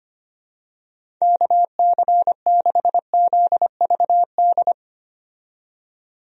Morse code, heard as a keyed tone, about 25 wpm, 700 Hz. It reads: KC6ZVD
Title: USA Call Signs - 6 Characters - 25wpm